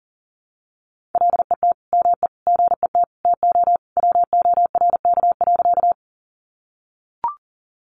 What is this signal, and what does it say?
Morse code, heard as a keyed tone, about 40 wpm, 700 Hz. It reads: LET ME GET TO WORK.
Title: Sentences from Top 300 Words 40wpm